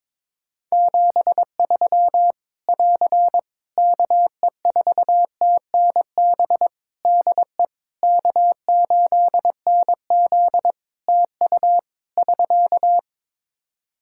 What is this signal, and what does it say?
Morse code, heard as a keyed tone, about 22 wpm, 700 Hz. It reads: 73 <AR> KE4TNB DE K8NZ TU <SK>
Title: Practice QSO - Exchange 4 - Ending - 22wpm